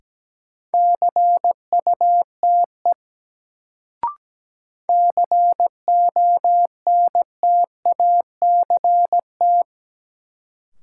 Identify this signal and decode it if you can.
Morse code, heard as a keyed tone, about 17 wpm, 700 Hz. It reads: CUTE CONTACT
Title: Sentences from Top 1000 Words - Review Part 7 - New Word Drill - 17wpm